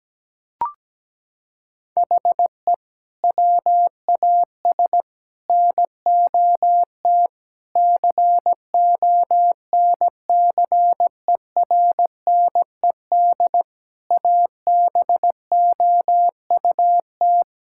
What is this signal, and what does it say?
Morse code, heard as a keyed tone, about 17 wpm, 700 Hz. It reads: HE WAS NOT CONCERNED ABOUT
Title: Sentences from Top 1000 Words - Review Part 6 - 17wpm